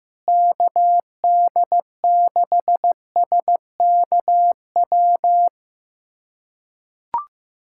Morse code, heard as a keyed tone, about 16 wpm, 700 Hz. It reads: KD6SKW
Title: USA Callsign DE Callsign - 6 Characters - 15wpm